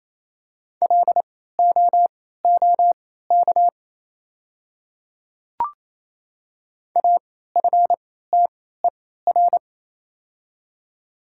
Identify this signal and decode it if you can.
Morse code, heard as a keyed tone, about 28 wpm, spaced as for 16 wpm, 700 Hz. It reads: LOOK AFTER
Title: 3x Character Spacing - Top 500 Words - 28wpm